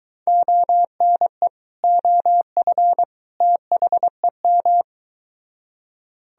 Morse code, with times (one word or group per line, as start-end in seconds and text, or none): 0.3-1.5 ONE
1.8-3.0 OF
3.4-4.8 THEM